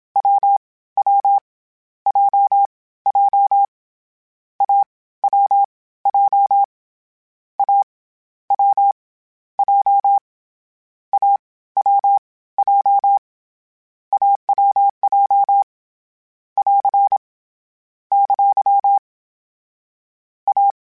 Morse code, heard as a keyed tone, about 27 wpm, 800 Hz. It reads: W W J J A W J A W J A W J AWJ <AR> ! A
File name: Difficult Combos / Difficult Combos-mp3s / AWJ1 @25